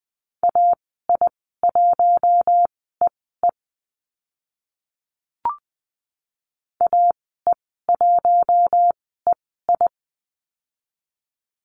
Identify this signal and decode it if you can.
Morse code, heard as a keyed tone, about 20 wpm, spaced as for 14 wpm, 700 Hz. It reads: AI1EE AE1EI
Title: 042 Lesson 8g - Number 1 - Call Signs 20wpm